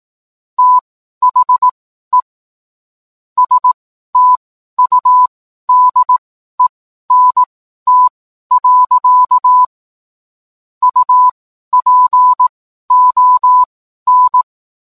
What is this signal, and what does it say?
Morse code, heard as a keyed tone, about 18 wpm, spaced as for 12 wpm, 1000 Hz. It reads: THE STUDENT. UPON
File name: cwblog / cw mp3 directory / mem morse 12